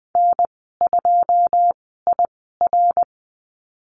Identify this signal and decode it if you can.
Morse code, heard as a keyed tone, about 20 wpm, spaced as for 14 wpm, 700 Hz. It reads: N2IR
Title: 088 Lesson 15g - Number 5 - Call Signs 20wpm